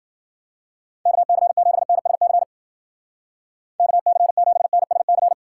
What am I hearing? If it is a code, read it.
Morse code, heard as a keyed tone, about 50 wpm, 700 Hz. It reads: KC7NSC KC7NSC